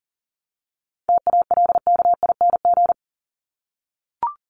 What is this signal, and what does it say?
Morse code, heard as a keyed tone, about 40 wpm, 700 Hz. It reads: TALKING
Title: ING Suffix - Top 100 Words with Base Words 40wpm